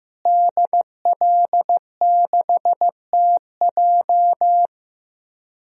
Morse code, heard as a keyed tone, about 15 wpm, 700 Hz. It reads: DL6TJ
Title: DXCC-100 Call Signs - 5 Characters - Repeated 3x - 15wpm